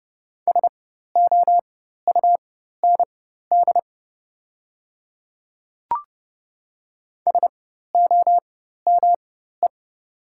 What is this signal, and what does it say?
Morse code, heard as a keyed tone, about 30 wpm, spaced as for 14 wpm, 700 Hz. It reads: SOUND SOME
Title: 4x Character Spacing - Top 100 Words - 30wpm